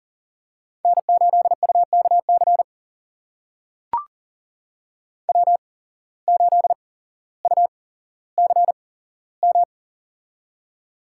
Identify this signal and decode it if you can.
Morse code, heard as a keyed tone, about 40 wpm, spaced as for 17 wpm, 700 Hz. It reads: N8UKC W8UCM
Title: USA Call Signs - 5 Characters - 8x Character Spacing - 40wpm